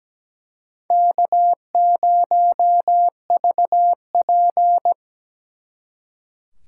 Morse code, heard as a keyed tone, about 20 wpm, 700 Hz. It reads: K0VP
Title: Speed Racing - USA 1x2 Call Signs 17wpm